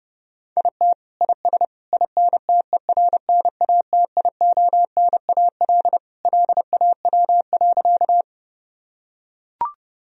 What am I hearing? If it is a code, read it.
Morse code, heard as a keyed tone, about 32 wpm, 700 Hz. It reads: IT IS INTERNATIONAL LAW.
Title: Sentences from Top 1000 Words - Review Part 4 - 30wpm